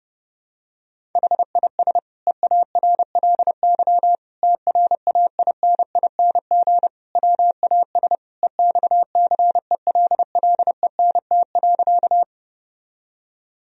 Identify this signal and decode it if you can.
Morse code, heard as a keyed tone, about 30 wpm, 700 Hz. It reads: HIS EARLY TRAINING WAS EXCELLENT.
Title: Sentences from Top 1000 Words - Review Part 4 - 30wpm